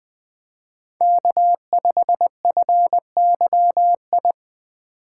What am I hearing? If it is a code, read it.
Morse code, heard as a keyed tone, about 21 wpm, 700 Hz. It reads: K5FYI
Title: Speed Racing - USA 1x3 Call Signs 17wpm